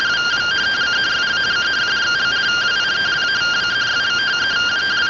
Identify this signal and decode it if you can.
Two-tone teleprinter signal, about 170 Hz shift.